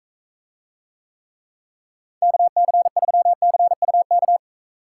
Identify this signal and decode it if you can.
Morse code, heard as a keyed tone, about 43 wpm, 700 Hz. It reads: KC3CUK